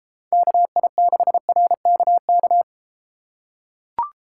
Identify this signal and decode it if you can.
Morse code, heard as a keyed tone, about 33 wpm, 700 Hz. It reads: KI6RKK